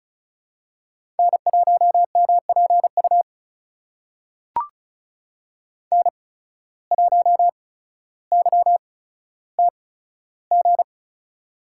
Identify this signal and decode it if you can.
Morse code, heard as a keyed tone, about 36 wpm, spaced as for 15 wpm, 700 Hz. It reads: N1MPU N1YTG
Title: USA Call Signs - 5 Characters - 8x Character Spacing - 35wpm